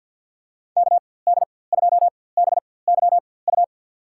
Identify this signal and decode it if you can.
Morse code, heard as a keyed tone, about 53 wpm, spaced as for 24 wpm, 700 Hz. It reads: KD2BYU
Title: USA Call Signs - 6 Characters - 4x Character Spacing - 50wpm